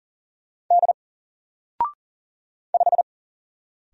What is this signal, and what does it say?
Morse code, heard as a keyed tone, about 40 wpm, 700 Hz. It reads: D 5
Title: Single Letter-Number 40wpm